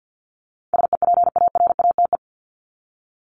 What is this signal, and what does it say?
Morse code, heard as a keyed tone, about 50 wpm, 700 Hz. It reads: SEPARATE
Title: Top 751-1000 Words 50wpm